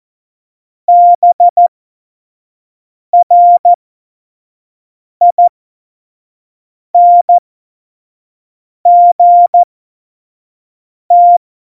Morse code, heard as a keyed tone, about 14 wpm, spaced as for 5 wpm, 700 Hz. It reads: BRINGT
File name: Morsekurs / Witze / Witz 03. November 2020, 25 F70